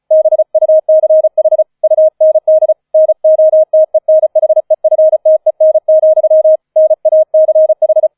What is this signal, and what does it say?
Morse code, heard as a keyed tone, about 34 wpm, 630 Hz. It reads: BUCH UND NOTENHEFTEN, NACH
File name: 004 Vierter Streich 35wpm